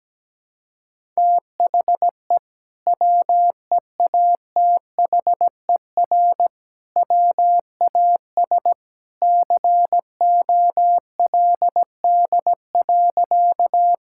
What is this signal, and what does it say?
Morse code, heard as a keyed tone, about 17 wpm, 700 Hz. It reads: THE WEATHER WAS COLD.
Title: Sentences from Top 1000 Words - Review Part 5 - 17wpm